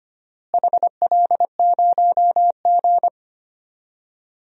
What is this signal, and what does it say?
Morse code, heard as a keyed tone, about 25 wpm, 700 Hz. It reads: HL0G
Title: DXCC-100 Call Signs - 4 Characters - Repeated 3x - 25wpm